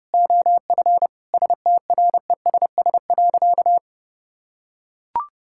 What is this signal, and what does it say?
Morse code, heard as a keyed tone, about 32 wpm, 700 Hz. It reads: OF STRESS.
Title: Sentences from Top 1000 Words - Review Part 6 - 30wpm